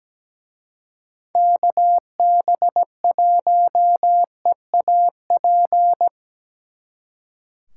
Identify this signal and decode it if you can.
Morse code, heard as a keyed tone, about 20 wpm, 700 Hz. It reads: KB1EAP
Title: Speed Racing - USA 2x3 Call Signs 17wpm